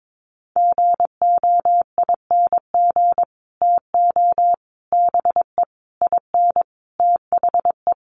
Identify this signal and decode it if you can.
Morse code, heard as a keyed tone, about 22 wpm, 700 Hz. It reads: GOING TO BE IN THE
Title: Sets of 5 Words Encore 22wpm